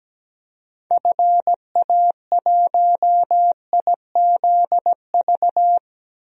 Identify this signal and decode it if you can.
Morse code, heard as a keyed tone, about 17 wpm, 700 Hz. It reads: FA1IZV
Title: DXCC-100 Call Signs - 6 Characters - Repeated 3x - 17wpm